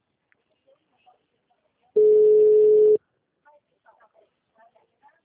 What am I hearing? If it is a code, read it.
Ringback tone (Europe, 425 Hz).